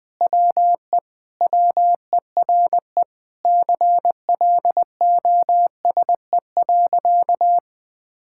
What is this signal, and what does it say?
Morse code, heard as a keyed tone, about 20 wpm, 700 Hz. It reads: WE WERE CLOSE.
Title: Sentences with 5-Word Phrases - 20wpm